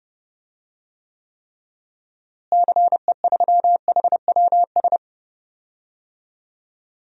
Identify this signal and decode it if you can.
Morse code, heard as a keyed tone, about 32 wpm, 700 Hz. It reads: CE3HWS
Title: DXCC-100 Call Signs - 6 Characters - Repeated 3x - 30wpm